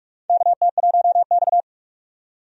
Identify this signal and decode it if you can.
Morse code, heard as a keyed tone, about 42 wpm, 700 Hz. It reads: KT1X
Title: USA Call Signs - 4 Characters - 45wpm